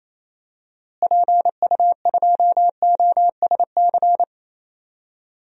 Morse code, heard as a keyed tone, about 28 wpm, 700 Hz. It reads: PU2OSC
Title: DXCC-100 Call Signs - 6 Characters - Repeated 2x - 28wpm